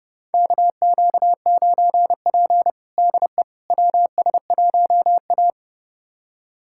Morse code, heard as a keyed tone, about 30 wpm, 700 Hz. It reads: KQ9P DE WS1A
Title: USA Callsign DE Callsign - 4 Characters - 30wpm